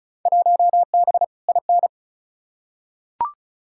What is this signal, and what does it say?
Morse code, heard as a keyed tone, about 36 wpm, 700 Hz. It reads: 1B IN
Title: Field Day Exchanges - 35wpm